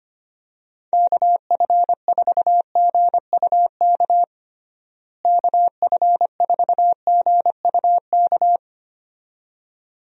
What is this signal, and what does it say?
Morse code, heard as a keyed tone, about 25 wpm, 700 Hz. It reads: KF4GUK KF4GUK
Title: USA Call Signs - 6 Characters - Repeated 2x - 25wpm